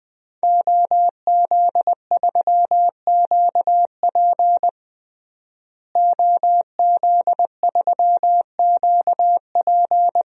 Morse code, heard as a keyed tone, about 20 wpm, 700 Hz. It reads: OZ3QP OZ3QP
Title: DXCC-100 Call Signs - 5 Characters - Repeated 3x - 20wpm